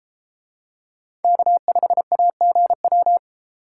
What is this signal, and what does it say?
Morse code, heard as a keyed tone, about 32 wpm, 700 Hz. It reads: K5AGW